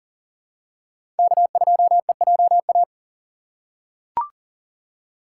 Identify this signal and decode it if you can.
Morse code, heard as a keyed tone, about 37 wpm, 700 Hz. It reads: K2EJA
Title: USA Call Signs - 5 Characters - 2x Character Spacing - 40wpm